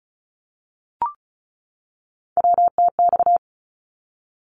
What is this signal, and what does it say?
Morse code, heard as a keyed tone, about 34 wpm, 700 Hz. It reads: WTX
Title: ARRL Sections 35wpm